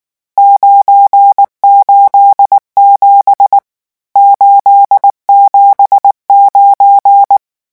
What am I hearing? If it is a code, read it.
Morse code, heard as a keyed tone, about 19 wpm, 800 Hz. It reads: 987 879
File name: morse mp3 / lecon12-vitesse 18